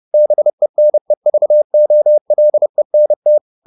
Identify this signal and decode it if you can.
Morse code, heard as a keyed tone, about 30 wpm, 600 Hz. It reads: BENEVOLENT
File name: the old man and the sea / the old man and the sea 1 30wpm